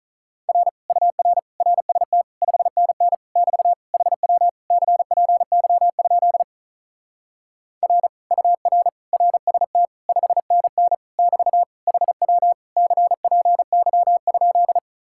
Morse code, heard as a keyed tone, about 44 wpm, 700 Hz. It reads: R UR RST 5NN <BT> HW CPY? R UR RST 5NN <BT> HW CPY?